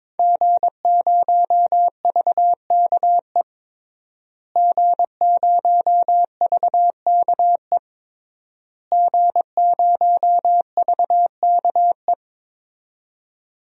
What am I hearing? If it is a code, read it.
Morse code, heard as a keyed tone, about 22 wpm, 700 Hz. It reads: G0VKE G0VKE G0VKE
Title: DXCC-100 Call Signs - 5 Characters - Repeated 3x - 22wpm